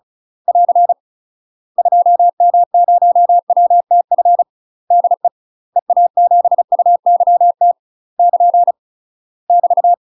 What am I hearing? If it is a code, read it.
Morse code, heard as a keyed tone, about 36 wpm, 700 Hz. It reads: <AR> 2M0WTF DE EA7UYT <KN> <BT>
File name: UK MASTER QSOs TRAINING / 35WPM / UK MASTER QSOs 0033